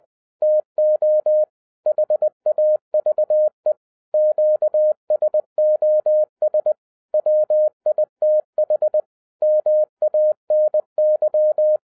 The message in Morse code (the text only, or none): TO HAVE QSOS WITH MANY